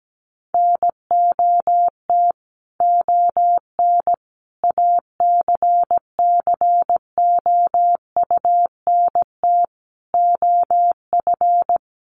Morse code, heard as a keyed tone, about 17 wpm, 700 Hz. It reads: NOT ON ACCOUNT OF
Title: Sentences from Top 1200 Words 17wpm